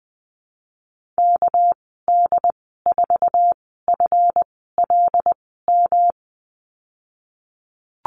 Morse code, heard as a keyed tone, about 20 wpm, spaced as for 15 wpm, 700 Hz. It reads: KD4FLM
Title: 200 Lesson 32g - Letter k - Call Signs 20wpm